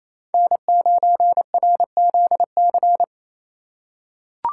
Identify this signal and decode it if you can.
Morse code, heard as a keyed tone, about 28 wpm, 700 Hz. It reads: N9RZC
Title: USA Call Signs - 5 Characters - 8x Character Spacing - 28wpm